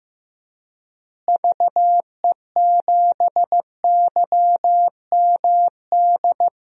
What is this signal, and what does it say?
Morse code, heard as a keyed tone, about 15 wpm, 700 Hz. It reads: VE7YMD
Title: DXCC-100 Call Signs - 6 Characters - Repeated 2x - 15wpm